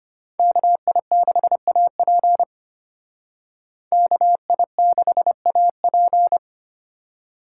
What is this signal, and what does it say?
Morse code, heard as a keyed tone, about 32 wpm, 700 Hz. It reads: KI6AP KI6AP